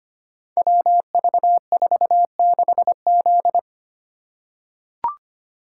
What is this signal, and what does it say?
Morse code, heard as a keyed tone, about 25 wpm, 700 Hz. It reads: WV46Z
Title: Code Groups - 5 Letter:Numbers - 25wpm